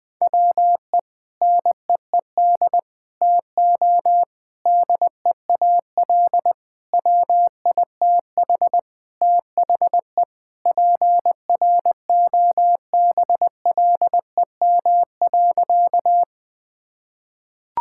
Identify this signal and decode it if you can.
Morse code, heard as a keyed tone, about 21 wpm, 700 Hz. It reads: WE NEED TO DEAL WITH THE PROBLEM.
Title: Sentences with 4-Word Phrases - 20wpm